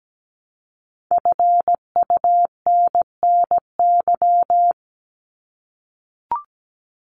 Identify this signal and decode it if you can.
Morse code, heard as a keyed tone, about 17 wpm, 700 Hz. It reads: FUNNY
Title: Top 1751-2000 Words 17wpm